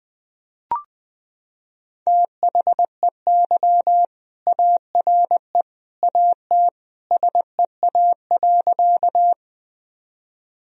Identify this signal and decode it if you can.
Morse code, heard as a keyed tone, about 20 wpm, 700 Hz. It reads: THEY ARE AT SEA.
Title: Sentences from Top 1000 Words - Review Part 1 - 20wpm